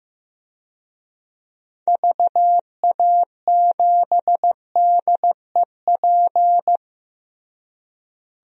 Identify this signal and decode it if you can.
Morse code, heard as a keyed tone, about 16 wpm, 700 Hz. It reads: VA7DEP
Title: DXCC-100 Call Signs - 6 Characters - Repeated 3x - 15wpm